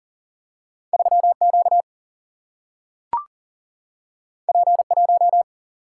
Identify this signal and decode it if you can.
Morse code, heard as a keyed tone, about 40 wpm, 700 Hz. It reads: 3 Q P 1
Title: Single-Letter Single-Number 40wpm